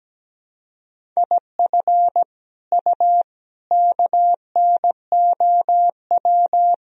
Morse code, heard as a keyed tone, about 17 wpm, 700 Hz. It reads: IF U KNOW
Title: Sentences from Top 400 Words and Abbreviations 17wpm